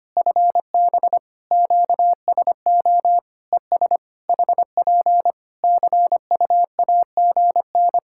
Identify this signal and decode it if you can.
Morse code, heard as a keyed tone, about 25 wpm, 700 Hz. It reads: FB QSO ES HP CUAGN
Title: Practice QSO - Exchange 4 - Ending - 25wpm